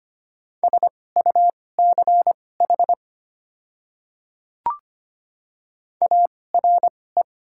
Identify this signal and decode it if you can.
Morse code, heard as a keyed tone, about 25 wpm, spaced as for 18 wpm, 700 Hz. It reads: SUCH ARE
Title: 2x Character Spacing - Top 500 Words - 25wpm